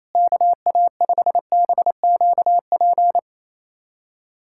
Morse code, heard as a keyed tone, about 28 wpm, 700 Hz. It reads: KA5BQP
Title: USA Callsign DE Callsign - 6 Characters - 28wpm